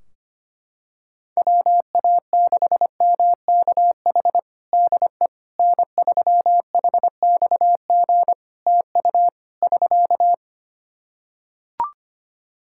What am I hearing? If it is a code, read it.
Morse code, heard as a keyed tone, about 25 wpm, 700 Hz. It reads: WA6MKH DE N3HXG TU <SK>
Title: Practice QSO - Exchange 4 - Ending - 25wpm